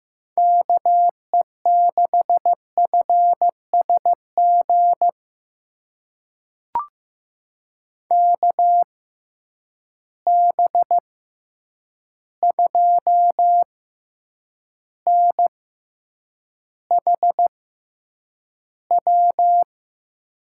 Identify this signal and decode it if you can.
Morse code, heard as a keyed tone, about 15 wpm, spaced as for 8 wpm, 700 Hz. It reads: KE6FSG KB2NHW
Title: USA Call Signs - 6 Characters - 6x Character Spacing - 15wpm